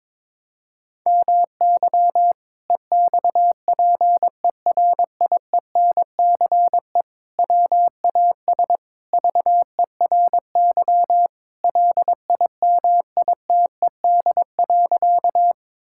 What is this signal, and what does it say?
Morse code, heard as a keyed tone, about 22 wpm, 700 Hz. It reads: MY EXPERIENCE WAS VERY LIMITED.
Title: Sentences from Top 1000 Words - Review Part 6 - 22wpm